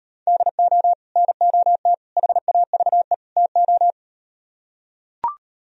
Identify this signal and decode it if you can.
Morse code, heard as a keyed tone, about 38 wpm, 700 Hz. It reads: DO NOT HAVE TO